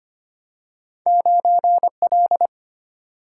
Morse code, heard as a keyed tone, about 25 wpm, 700 Hz. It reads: 9 L
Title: Single-Letter Single-Number 25wpm